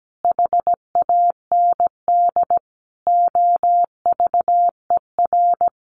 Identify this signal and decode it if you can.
Morse code, heard as a keyed tone, about 17 wpm, 700 Hz. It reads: HAND OVER
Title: Sentences from Top 400 Words 17wpm